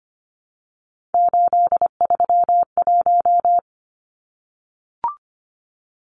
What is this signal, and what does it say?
Morse code, heard as a keyed tone, about 25 wpm, 700 Hz. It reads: 831
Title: Numbers - 3-Digit 25wpm